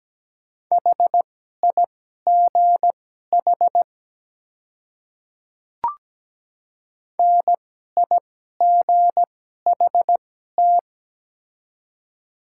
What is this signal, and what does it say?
Morse code, heard as a keyed tone, about 17 wpm, spaced as for 12 wpm, 700 Hz. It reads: HIGH NIGHT
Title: 2x Character Spacing - Top 300 Words - 17wpm